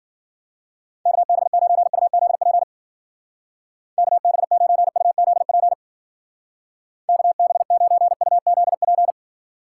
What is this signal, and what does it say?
Morse code, heard as a keyed tone, about 52 wpm, 700 Hz. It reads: KB9UZP KB9UZP KB9UZP